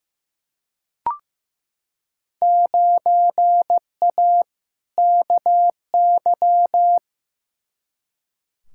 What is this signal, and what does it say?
Morse code, heard as a keyed tone, about 15 wpm, 700 Hz. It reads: 9A KY
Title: Field Day Exchanges Encore - 15wpm